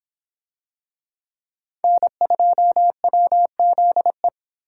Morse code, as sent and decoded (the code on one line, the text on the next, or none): -. ..--- .-- --.. .
N2WZE